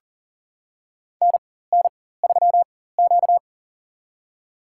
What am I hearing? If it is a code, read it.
Morse code, heard as a keyed tone, about 42 wpm, spaced as for 19 wpm, 690 Hz. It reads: NN3Q
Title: USA Call Signs - 4x Character Spacing - 40wpm